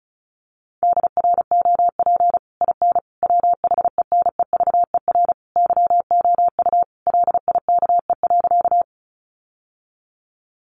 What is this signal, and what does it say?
Morse code, heard as a keyed tone, about 34 wpm, 700 Hz. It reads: DROP IN WHENEVER YOU LIKE.
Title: Sentences from Top 1400 Words 35wpm